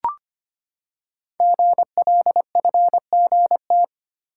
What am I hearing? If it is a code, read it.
Morse code, heard as a keyed tone, about 25 wpm, 700 Hz. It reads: GLFGT
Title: Code Groups - 5 Letters - 25wpm